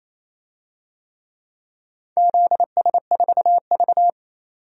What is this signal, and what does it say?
Morse code, heard as a keyed tone, about 28 wpm, 700 Hz. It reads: ZS4V
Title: DXCC-100 Call Signs - 4 Characters - 28wpm